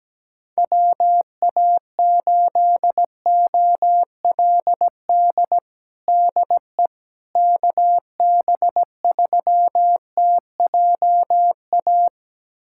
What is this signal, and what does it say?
Morse code, heard as a keyed tone, about 17 wpm, 700 Hz. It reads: WA8OLD DE KB3TJA
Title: USA Callsign DE Callsign - 6 Characters - 17wpm